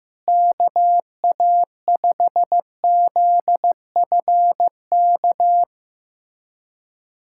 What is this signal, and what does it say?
Morse code, heard as a keyed tone, about 16 wpm, 700 Hz. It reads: KA5ZFK
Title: USA Callsign DE Callsign - 6 Characters - 15wpm